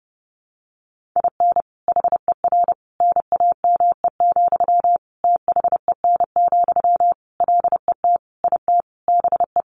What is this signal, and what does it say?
Morse code, heard as a keyed tone, about 30 wpm, 700 Hz. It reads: IN HER NAME, THEN, LET IT BE
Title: Sentences from Top 500 Words 30wpm